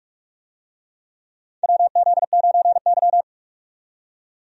Morse code, heard as a keyed tone, about 46 wpm, 700 Hz. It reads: WZ9Y
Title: USA Call Signs - 4 Characters - 45wpm